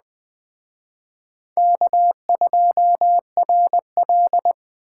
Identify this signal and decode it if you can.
Morse code, heard as a keyed tone, about 21 wpm, 700 Hz. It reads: K2RL
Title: Speed Racing - USA 1x2 Call Signs 17wpm